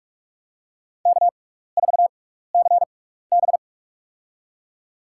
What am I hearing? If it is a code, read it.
Morse code, heard as a keyed tone, about 43 wpm, spaced as for 15 wpm, 700 Hz. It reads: K4CB
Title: USA Call Signs - 4 Characters - 6x Character Spacing - 45wpm